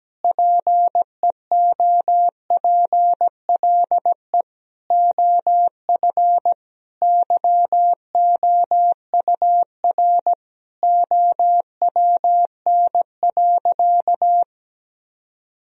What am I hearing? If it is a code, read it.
Morse code, heard as a keyed tone, about 17 wpm, 700 Hz. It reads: PEOPLE OF YOUR OWN.
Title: Sentences from Top 1000 Words - Review Part 2 - 17wpm